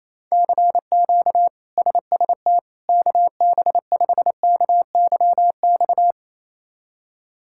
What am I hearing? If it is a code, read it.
Morse code, heard as a keyed tone, about 28 wpm, 700 Hz. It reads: CQ SST KB5KYX
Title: K1USN SST QSO Practice - 28wpm